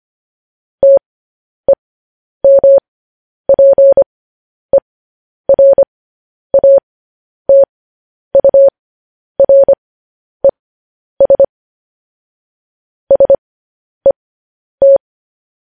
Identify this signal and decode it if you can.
Morse code, heard as a keyed tone, about 25 wpm, spaced as for 10 wpm, 550 Hz. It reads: TEMPERATURES SET
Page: news headlines in morse code at 10 wpm